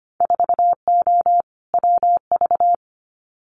Morse code, heard as a keyed tone, about 25 wpm, 700 Hz. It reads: 4O WV
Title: Winter Field Day Exchanges Encore 25wpm